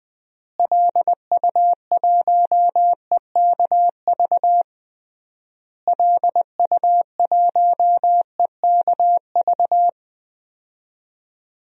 Morse code, heard as a keyed tone, about 20 wpm, 700 Hz. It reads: LU1EKV LU1EKV
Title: DXCC-100 Call Signs - 6 Characters - Repeated 3x - 20wpm